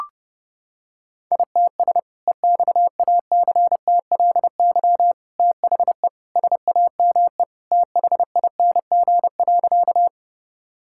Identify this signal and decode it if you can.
Morse code, heard as a keyed tone, about 32 wpm, 700 Hz. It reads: ITS EXACTLY THE SAME THING.
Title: Sentences from Top 1000 Words - Review Part 6 - 30wpm